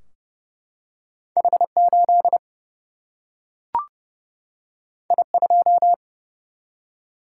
Morse code, heard as a keyed tone, about 30 wpm, 700 Hz. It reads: H8 I2
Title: Single-Letter Single-Number 30wpm